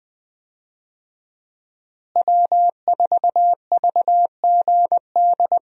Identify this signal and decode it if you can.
Morse code, heard as a keyed tone, about 21 wpm, 700 Hz. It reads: W4VGD
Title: USA Call Signs - 5 Characters - Repeated 3x - 20wpm